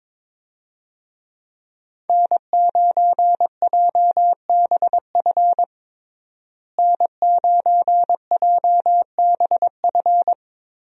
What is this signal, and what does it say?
Morse code, heard as a keyed tone, about 22 wpm, 700 Hz. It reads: N9JBF N9JBF
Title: USA Call Signs - 5 Characters - Repeated 2x - 22wpm